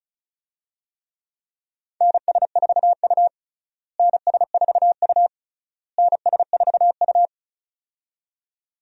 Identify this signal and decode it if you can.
Morse code, heard as a keyed tone, about 36 wpm, 700 Hz. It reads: NS4U NS4U NS4U
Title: USA Call Signs - 4 Characters - Repeated 3x - 35wpm